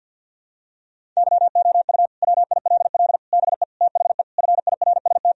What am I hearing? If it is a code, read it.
Morse code, heard as a keyed tone, about 50 wpm, 700 Hz. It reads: YOU WILL BE THE FIRST
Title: Sentences from Top 1000 Words - Review Part 4 - 50wpm